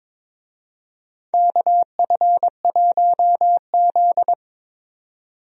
Morse code, heard as a keyed tone, about 22 wpm, 700 Hz. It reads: KF1Z
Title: USA Call Signs - 4 Characters - 22wpm